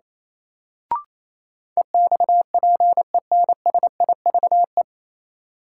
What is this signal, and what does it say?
Morse code, heard as a keyed tone, about 28 wpm, 700 Hz. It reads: EXPENSIVE
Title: Sentences from Top 1500 Words - New Word Drill Alt - 28wpm